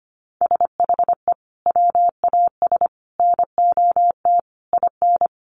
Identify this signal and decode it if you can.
Morse code, heard as a keyed tone, about 25 wpm, 700 Hz. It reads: SHE WAS NOT IN
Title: Sentences from Top 900 Words 25wpm